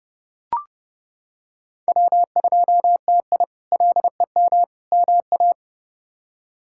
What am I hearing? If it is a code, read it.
Morse code, heard as a keyed tone, about 30 wpm, 700 Hz. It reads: W2TI LEM MA
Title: K1USN SST QSO Practice - 30wpm